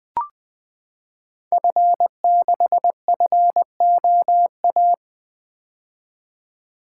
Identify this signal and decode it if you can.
Morse code, heard as a keyed tone, about 20 wpm, 700 Hz. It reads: F6FOA
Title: Code Groups - 5 Letter:Numbers - 20wpm